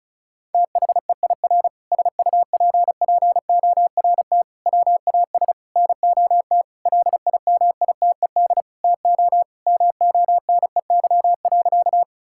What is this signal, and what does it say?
Morse code, heard as a keyed tone, about 34 wpm, 700 Hz. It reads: THEIR SUPPORT WAS NOT LIMITED TO MONEY.
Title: Sentences from Top 1000 Words - Review Part 6 - 35wpm